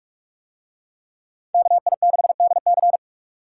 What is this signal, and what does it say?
Morse code, heard as a keyed tone, about 41 wpm, 700 Hz. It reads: KI6DC